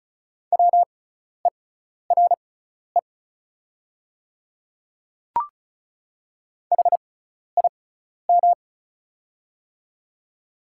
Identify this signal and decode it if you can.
Morse code, heard as a keyed tone, about 36 wpm, spaced as for 12 wpm, 690 Hz. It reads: WERE HIM
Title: 6x Character Spacing - Top 100 Words - 35wpm